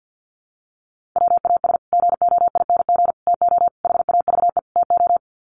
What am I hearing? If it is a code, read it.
Morse code, heard as a keyed tone, about 50 wpm, 700 Hz. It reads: WAS GOING TO HAVE TO
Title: Sets of 5 Words Encore 50wpm